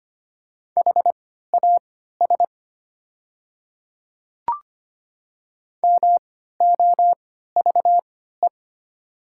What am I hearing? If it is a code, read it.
Morse code, heard as a keyed tone, about 25 wpm, spaced as for 14 wpm, 700 Hz. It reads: HAS MOVE
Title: 3x Character Spacing - Top 200 Words - 25wpm